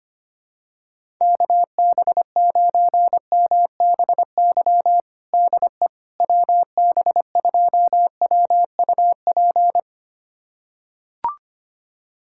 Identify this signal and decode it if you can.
Morse code, heard as a keyed tone, about 25 wpm, 700 Hz. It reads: KB9MBY DE WB2WUP
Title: USA Callsign DE Callsign - 6 Characters - 25wpm